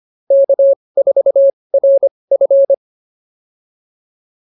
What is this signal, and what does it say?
Morse code, heard as a keyed tone, about 25 wpm, spaced as for 20 wpm, 550 Hz. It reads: K4RF